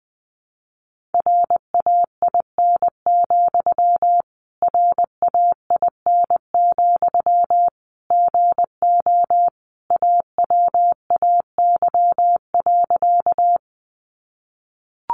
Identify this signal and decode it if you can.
Morse code, heard as a keyed tone, about 20 wpm, 700 Hz. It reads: RAIN, RAIN, GO AWAY.
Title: Sentences from Top 600 Words 20wpm